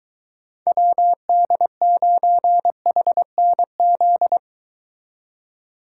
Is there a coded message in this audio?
Yes: Morse code from 0.7 to 4.4 s, heard as a keyed tone, about 23 wpm, 700 Hz. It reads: WD9HNZ